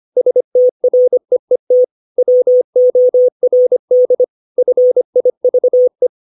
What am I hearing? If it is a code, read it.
Morse code, heard as a keyed tone, about 25 wpm, 500 Hz. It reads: STREET WORD FIVE
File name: 632 words / 632 Words 25WPM